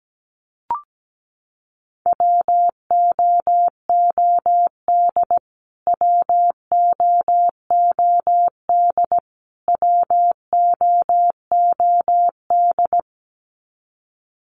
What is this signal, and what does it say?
Morse code, heard as a keyed tone, about 17 wpm, 700 Hz. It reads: WOOD WOOD WOOD
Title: Top 300 Words Repeated 17wpm